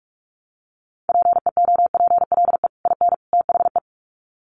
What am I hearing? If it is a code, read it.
Morse code, heard as a keyed tone, about 44 wpm, 700 Hz. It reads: PEOPLE IN THE
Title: Sets of 3 Words 45wpm